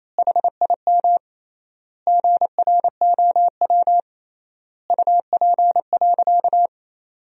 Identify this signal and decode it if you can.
Morse code, heard as a keyed tone, about 28 wpm, 700 Hz. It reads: HIM GROW UP.
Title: 3x Word-Spacing - Sentences from Top 300 Words - 28wpm